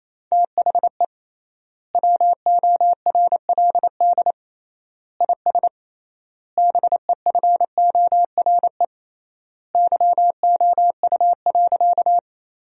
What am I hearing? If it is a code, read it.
Morse code, heard as a keyed tone, about 28 wpm, 700 Hz. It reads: THE WORLD IS BEFORE YOU.
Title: 3x Word-Spacing - Sentences from Top 200 Words - 28wpm